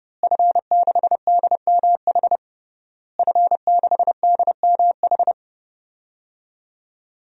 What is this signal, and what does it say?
Morse code, heard as a keyed tone, about 30 wpm, 700 Hz. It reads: F6DMH F6DMH
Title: DXCC-100 Call Signs - 5 Characters - Repeated 3x - 30wpm